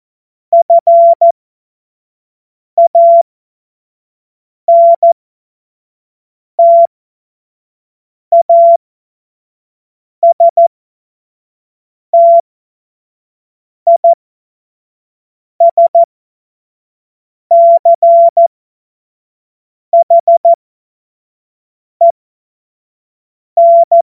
Morse code, heard as a keyed tone, about 14 wpm, spaced as for 5 wpm, 680 Hz. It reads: FANTASTISCHEN
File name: Morsekurs / Witze / Witz 12. April 2022 25 F70